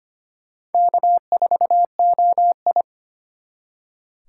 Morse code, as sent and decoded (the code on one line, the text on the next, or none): -.- ....- --- ..
K4OI